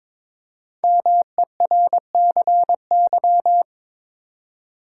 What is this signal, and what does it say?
Morse code, heard as a keyed tone, about 22 wpm, 690 Hz. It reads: MERCY
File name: 5 Letter Words - First 3 Letters Given - 22wpm